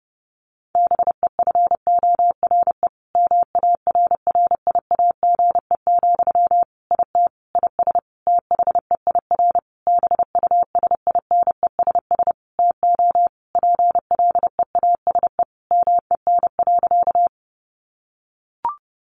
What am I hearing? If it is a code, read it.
Morse code, heard as a keyed tone, about 30 wpm, 700 Hz. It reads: BEFORE MARRIAGE, IT IS THEIR BUSINESS TO PLEASE MEN.
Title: Sentences from Top 1500 Words 30wpm